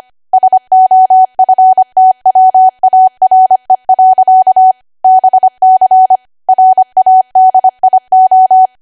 Morse code, heard as a keyed tone, about 25 wpm, 750 Hz. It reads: SOFTWARE. BC RADIO
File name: Archive / 25 WPM / 25WPM